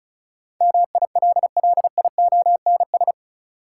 Morse code, heard as a keyed tone, about 36 wpm, 700 Hz. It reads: MILLIONS
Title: Sets of 5 Words - New Word Drill - 35wpm